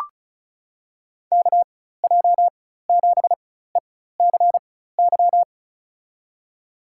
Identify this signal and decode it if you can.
Morse code, heard as a keyed tone, about 36 wpm, spaced as for 16 wpm, 690 Hz. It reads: KJ7ECY
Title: USA Call Signs - 6 Characters - 4x Character Spacing - 35wpm